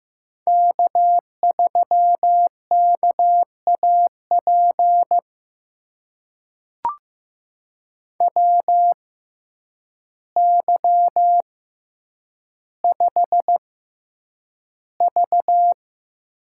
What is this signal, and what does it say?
Morse code, heard as a keyed tone, about 15 wpm, spaced as for 8 wpm, 700 Hz. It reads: K3KAP WY5V
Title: USA Call Signs - 6x Character Spacing - 15wpm